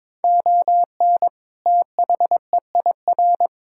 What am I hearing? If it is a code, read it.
Morse code, heard as a keyed tone, about 22 wpm, 700 Hz. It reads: ON THEIR
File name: Build A Phrase - 2-Word Phrases - 22wpm